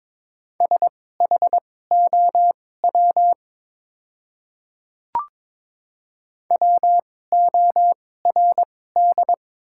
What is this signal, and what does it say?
Morse code, heard as a keyed tone, about 22 wpm, spaced as for 16 wpm, 700 Hz. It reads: SHOW WORD
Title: 2x Character Spacing - Top 200 Words - 22wpm